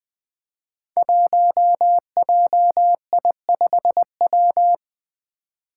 Morse code, heard as a keyed tone, about 20 wpm, 700 Hz. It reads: 1JI5W
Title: Five Challenging Letter-Number Combinations 20wpm